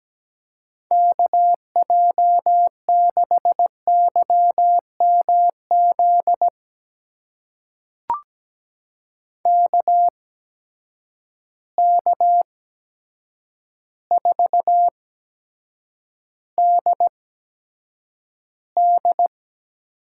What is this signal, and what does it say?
Morse code, heard as a keyed tone, about 17 wpm, spaced as for 8 wpm, 700 Hz. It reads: KJ6YMZ KK4DD
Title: USA Call Signs - 6 Characters - 8x Character Spacing - 17wpm